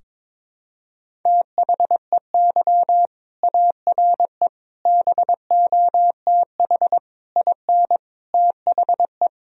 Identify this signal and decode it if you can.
Morse code, heard as a keyed tone, about 23 wpm, 700 Hz. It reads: THEY ARE BOTH IN THE
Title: Sentences from Top 1000 Words - Review Part 1 - 22wpm